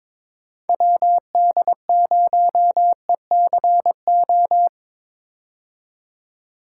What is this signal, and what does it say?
Morse code, heard as a keyed tone, about 22 wpm, 700 Hz. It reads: WD0ECO
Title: K1USN SST QSO Practice - 22wpm